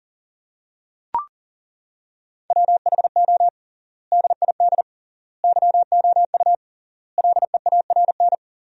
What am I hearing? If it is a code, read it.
Morse code, heard as a keyed tone, about 40 wpm, 700 Hz. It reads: WHO DID YOU LEARN
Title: 3x Word-Spacing - Sentences from Top 300 Words - 40wpm